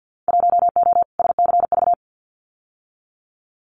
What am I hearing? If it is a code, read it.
Morse code, heard as a keyed tone, about 50 wpm, 700 Hz. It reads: 1O SCV
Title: Winter Field Day Exchanges 50wpm